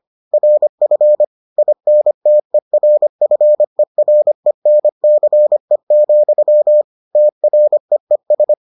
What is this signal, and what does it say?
Morse code, heard as a keyed tone, about 25 wpm, 600 Hz. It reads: RF INTERFERENCE, TREES